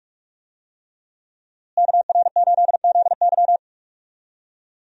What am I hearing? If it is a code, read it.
Morse code, heard as a keyed tone, about 44 wpm, 700 Hz. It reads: KR8ZY
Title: USA Call Signs - 5 Characters - Repeated 3x - 45wpm